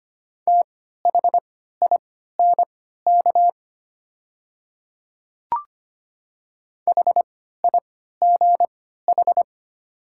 Morse code, heard as a keyed tone, about 25 wpm, spaced as for 14 wpm, 690 Hz. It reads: THINK HIGH
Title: 3x Character Spacing - Top 300 Words - 25wpm